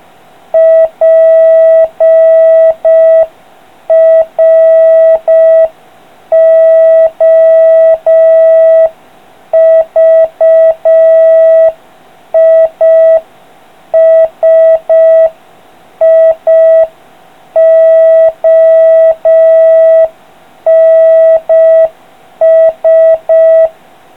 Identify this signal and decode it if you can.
Morse code, heard as a keyed tone, about 5 wpm, 3250 Hz, with noise. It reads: PROVISIONS